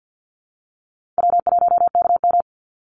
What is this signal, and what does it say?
Morse code, heard as a keyed tone, about 50 wpm, 700 Hz. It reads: W1KM
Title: Call Signs Repeated 2x - 50wpm